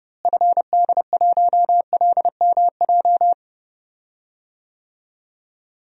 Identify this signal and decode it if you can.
Morse code, heard as a keyed tone, about 32 wpm, 700 Hz. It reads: FD1LMJ
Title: DXCC-100 Call Signs - 6 Characters - Repeated 2x - 30wpm